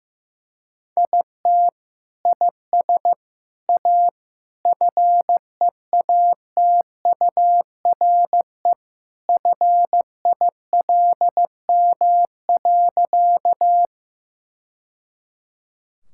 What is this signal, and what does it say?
Morse code, heard as a keyed tone, about 15 wpm, 700 Hz. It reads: IT IS A FEATURE FILM.
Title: Sentences from Top 1000 Words - Review Part 5 - 15wpm